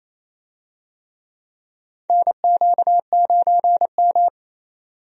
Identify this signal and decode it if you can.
Morse code, heard as a keyed tone, about 28 wpm, 690 Hz. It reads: NQ9M
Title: USA Call Signs - 4 Characters - 28wpm